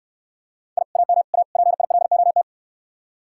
Morse code, heard as a keyed tone, about 69 wpm, 700 Hz. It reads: I AM A PILOT